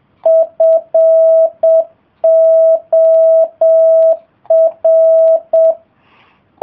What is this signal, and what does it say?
Morse code, heard as a keyed tone, about 7 wpm, 660 Hz. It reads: FOR